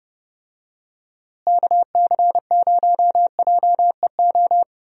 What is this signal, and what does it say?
Morse code, heard as a keyed tone, about 31 wpm, 700 Hz. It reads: KC0JEO
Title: USA Call Signs - 6 Characters - Repeated 3x - 30wpm